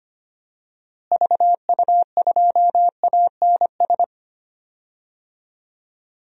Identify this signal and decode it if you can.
Morse code, heard as a keyed tone, about 25 wpm, 700 Hz. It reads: VU2ANS
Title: DXCC-100 Call Signs - 6 Characters - Repeated 3x - 25wpm